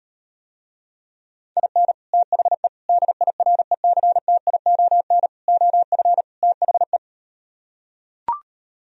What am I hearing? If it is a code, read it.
Morse code, heard as a keyed tone, about 38 wpm, 700 Hz. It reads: IN THE DIRECTION OF THE